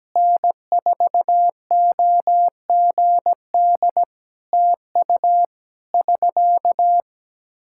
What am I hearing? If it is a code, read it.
Morse code, heard as a keyed tone, about 17 wpm, 700 Hz. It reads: N4OGD TU <SK>
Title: K1USN SST QSO Practice 17wpm